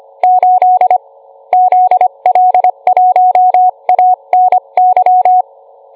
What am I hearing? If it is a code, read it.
Morse code, heard as a keyed tone, about 25 wpm, 730 Hz. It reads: 8 ZL1ANY